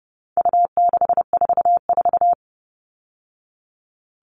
Morse code, heard as a keyed tone, about 30 wpm, 700 Hz. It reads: U644
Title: All Cut Numbers - 4-Digit 30wpm